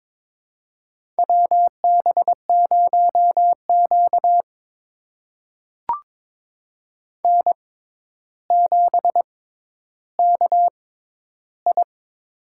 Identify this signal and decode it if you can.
Morse code, heard as a keyed tone, about 22 wpm, spaced as for 11 wpm, 700 Hz. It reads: WB0Q N7KI
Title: USA Call Signs - 4 Characters - 6x Character Spacing - 22wpm